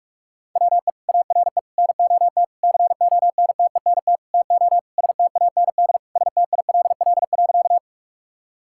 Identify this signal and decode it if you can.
Morse code, heard as a keyed tone, about 41 wpm, 700 Hz. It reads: WE ARE NOT CONTENT TO STAND STILL.
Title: Sentences from Top 1000 Words - Review Part 6 - 45wpm